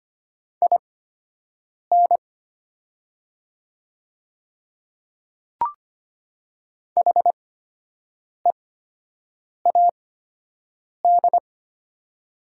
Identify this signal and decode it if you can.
Morse code, heard as a keyed tone, about 25 wpm, spaced as for 7 wpm, 700 Hz. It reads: IN HEAD
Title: 8x Character Spacing - Top 300 Words - 25wpm